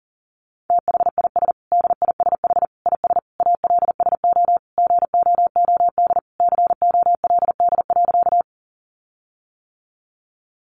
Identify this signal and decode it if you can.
Morse code, heard as a keyed tone, about 40 wpm, 700 Hz. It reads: THIS DISH IS ALSO GOOD COLD.
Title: Sentences from Top 1300 Words 40wpm